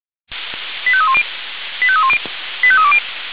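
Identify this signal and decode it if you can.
Five-tone selective call (ZVEI), heard as quick runs of five tones, tones of about 70 ms. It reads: 85310 85310 85320
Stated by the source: ZVEI five-tone selective call received on SDR, from audio.f85360.feuerwehr-orlamuende.org (Feuerwehr Orlamünde fire service).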